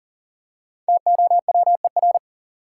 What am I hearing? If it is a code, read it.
Morse code, heard as a keyed tone, about 40 wpm, 700 Hz. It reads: TOWER
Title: Five Letter Words - 40wpm